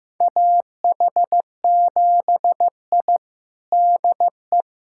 Morse code, heard as a keyed tone, about 16 wpm, 700 Hz. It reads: AH7I DE
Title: USA Callsign DE Callsign - 4 Characters - 15wpm